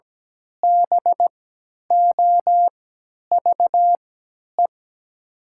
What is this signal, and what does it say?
Morse code, heard as a keyed tone, about 17 wpm, spaced as for 10 wpm, 700 Hz. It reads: BOVE
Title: 3x Character Spacing - Top 500 Words - 17wpm